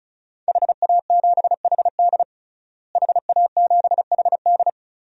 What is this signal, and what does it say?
Morse code, heard as a keyed tone, about 36 wpm, 700 Hz. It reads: HA7HD HA7HD
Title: DXCC-100 Call Signs - 5 Characters - Repeated 3x - 35wpm